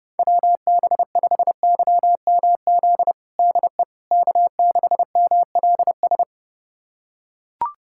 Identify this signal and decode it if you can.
Morse code, heard as a keyed tone, about 30 wpm, 700 Hz. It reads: WB5YMZ DE K6MLS
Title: USA Callsign DE Callsign 30wpm